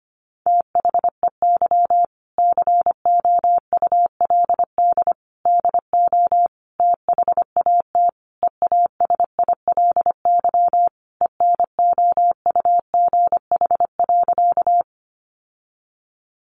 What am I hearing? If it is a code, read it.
Morse code, heard as a keyed tone, about 25 wpm, 700 Hz. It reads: THEY COULD DO THAT EASILY ENOUGH.
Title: Sentences from Top 900 Words 25wpm